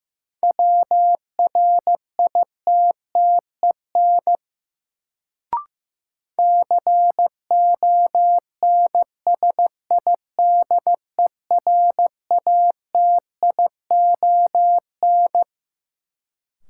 Morse code, heard as a keyed tone, about 15 wpm, 700 Hz. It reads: WRITTEN CONSIDERATION
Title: Sentences from Top 1500 Words - New Word Drill - 15wpm